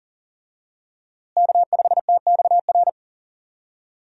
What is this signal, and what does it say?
Morse code, heard as a keyed tone, about 40 wpm, 700 Hz. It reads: K5TXR
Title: USA Call Signs - 5 Characters - Repeated 2x - 40wpm